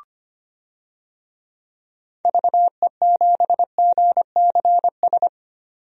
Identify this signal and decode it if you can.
Morse code, heard as a keyed tone, about 25 wpm, 700 Hz. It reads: VE7GCS
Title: DXCC-100 Call Signs - 6 Characters - 25wpm